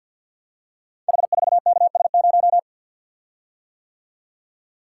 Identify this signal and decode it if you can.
Morse code, heard as a keyed tone, about 52 wpm, 700 Hz. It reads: H4KS0
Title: Code Groups - 5 Letter:Numbers - 50wpm